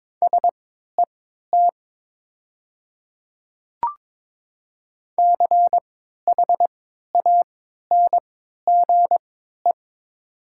Morse code, heard as a keyed tone, about 22 wpm, spaced as for 12 wpm, 700 Hz. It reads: SET CHANGE
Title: 3x Character Spacing - Top 200 Words - 22wpm